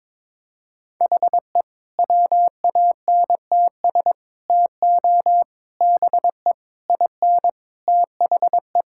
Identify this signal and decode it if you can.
Morse code, heard as a keyed tone, about 22 wpm, 700 Hz. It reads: HE WANTS TO BE IN THE
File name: practice / 22wpm / Sentences with 4-Word Phrases - 22wpm